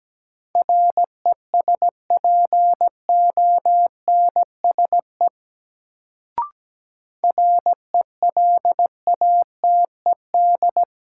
Morse code, heard as a keyed tone, about 17 wpm, 700 Hz. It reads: RESPONSE RELATED
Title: Sentences from Top 1400 Words - New Word Drill Alt - 17wpm